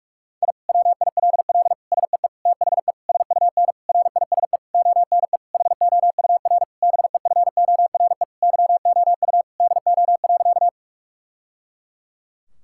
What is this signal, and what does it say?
Morse code, heard as a keyed tone, about 46 wpm, 700 Hz. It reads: I WILL SEE THE SUN RISE ONE HOUR BEFORE YOU DO.
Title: Sentences from Top 1000 Words - Review Part 6 - 45wpm